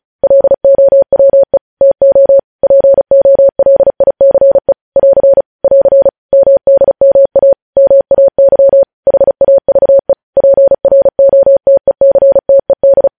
Morse code, heard as a keyed tone, about 36 wpm, 550 Hz. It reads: LOWE TO POLICE <AR> <AR> MDMA MAY HAVE PROTECTED
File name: podcast / News headlines-1x-35x35